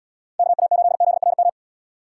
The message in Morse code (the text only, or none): KA8ZRM